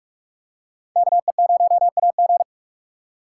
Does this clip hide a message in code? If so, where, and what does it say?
Yes: Morse code from 1.0 to 2.4 s, heard as a keyed tone, about 44 wpm, 700 Hz. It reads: KE0AG